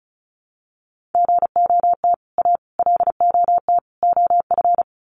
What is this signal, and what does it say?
Morse code, heard as a keyed tone, about 34 wpm, 700 Hz. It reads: GOT A LOT OF
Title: Sets of 4 Words 35wpm